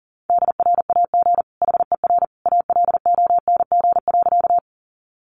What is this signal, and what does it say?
Morse code, heard as a keyed tone, about 40 wpm, 700 Hz. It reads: DRAG HER ALONG.
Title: Sentences from Top 1500 Words 40wpm